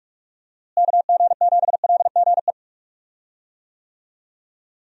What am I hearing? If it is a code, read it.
Morse code, heard as a keyed tone, about 46 wpm, 700 Hz. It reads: KG7LGE
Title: USA Call Signs - 6 Characters - Repeated 3x - 45wpm